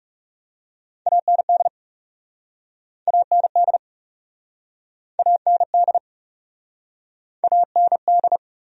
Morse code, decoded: AND AND AND AND

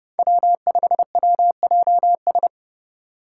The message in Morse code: W5WJS

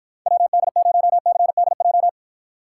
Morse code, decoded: WD0CDJ

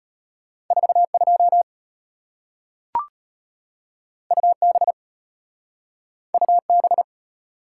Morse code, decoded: 42 UB UB

